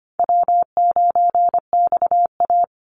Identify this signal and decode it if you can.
Morse code, heard as a keyed tone, about 25 wpm, 700 Hz. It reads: W9XA
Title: Easy Call Signs Repeated 3x - 25wpm